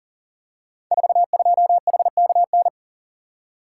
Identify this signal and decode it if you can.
Morse code, heard as a keyed tone, about 43 wpm, 700 Hz. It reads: 42HKN
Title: Code Groups - 5 Letter:Numbers - 40wpm